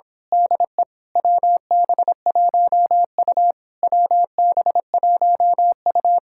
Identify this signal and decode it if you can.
Morse code, heard as a keyed tone, about 26 wpm, 700 Hz. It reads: DE WB1U WB1U
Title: Speed-Racing Common QSO Phrases - 22wpm